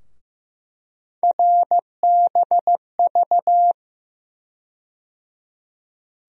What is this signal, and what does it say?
Morse code, heard as a keyed tone, about 15 wpm, 700 Hz. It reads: RBV
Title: USA Call Sign Postfixes - 3 Letters 15wpm